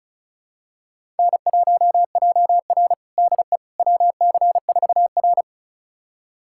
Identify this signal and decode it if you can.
Morse code, heard as a keyed tone, about 34 wpm, 700 Hz. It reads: N1JR DE WC4R
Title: USA Callsign DE Callsign - 4 Characters - 35wpm